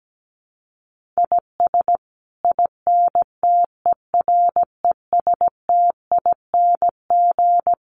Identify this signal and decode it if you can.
Morse code, heard as a keyed tone, about 17 wpm, 700 Hz. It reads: IS INTERESTING
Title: Sets of 5 Words Encore2 17wpm